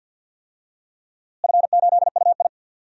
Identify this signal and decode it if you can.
Morse code, heard as a keyed tone, about 47 wpm, 700 Hz. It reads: F8UI
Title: DXCC-100 Call Signs - 4 Characters - Repeated 2x - 50wpm